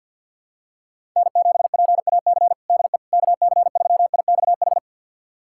Morse code, heard as a keyed tone, about 50 wpm, 700 Hz. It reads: N7PAC DE KC3IXH
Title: USA Callsign DE Callsign 50wpm